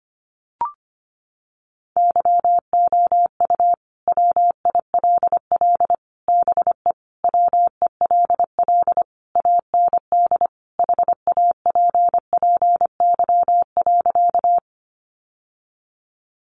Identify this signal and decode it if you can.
Morse code, heard as a keyed tone, about 25 wpm, 700 Hz. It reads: YOU WILL BE WELL AND HAPPY.
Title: Sentences from Top 1000 Words 25wpm